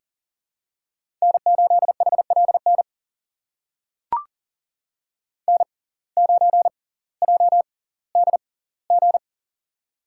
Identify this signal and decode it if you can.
Morse code, heard as a keyed tone, about 40 wpm, spaced as for 20 wpm, 700 Hz. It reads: N8HLN N9JDG
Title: USA Call Signs - 5 Characters - 6x Character Spacing - 40wpm